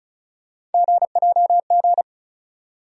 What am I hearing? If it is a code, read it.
Morse code, heard as a keyed tone, about 34 wpm, 700 Hz. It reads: GJG
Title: USA Call Sign Postfixes - 3 Letters 35wpm